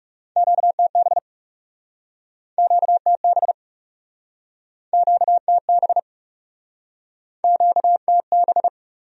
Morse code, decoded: QTB QTB QTB QTB

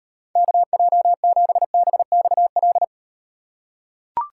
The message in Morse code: KJ7BXL

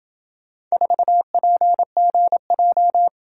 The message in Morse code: 4PGJ

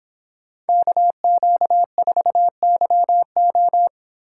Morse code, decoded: KQ4YO

